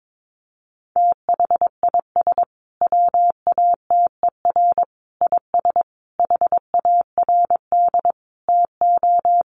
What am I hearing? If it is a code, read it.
Morse code, heard as a keyed tone, about 22 wpm, 700 Hz. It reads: THIS WATER IS HARD TO
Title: Sentences from Top 1300 Words 22wpm